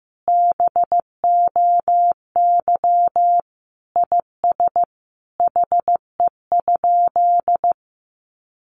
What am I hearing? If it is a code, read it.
Morse code, heard as a keyed tone, about 15 wpm, 700 Hz. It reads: BOY IS HE?
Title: Sentences from Top 1300 Words 15wpm